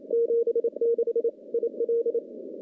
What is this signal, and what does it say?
Morse code, heard as a keyed tone, about 27 wpm, 470 Hz. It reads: 76 IL
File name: contests / CW-SS-k5tr- -1300